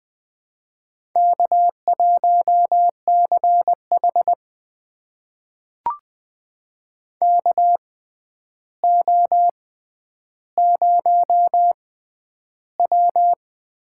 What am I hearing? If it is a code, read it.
Morse code, heard as a keyed tone, about 20 wpm, spaced as for 10 wpm, 700 Hz. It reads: K1CH KO0W
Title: USA Call Signs - 4 Characters - 6x Character Spacing - 20wpm